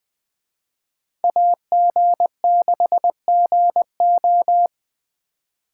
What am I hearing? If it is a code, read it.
Morse code, heard as a keyed tone, about 21 wpm, 700 Hz. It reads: AG6GO